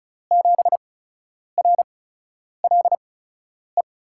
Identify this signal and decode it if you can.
Morse code, heard as a keyed tone, about 36 wpm, spaced as for 10 wpm, 700 Hz. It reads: 7RLE
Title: USA Call Signs - 6 Characters - 8x Character Spacing - 35wpm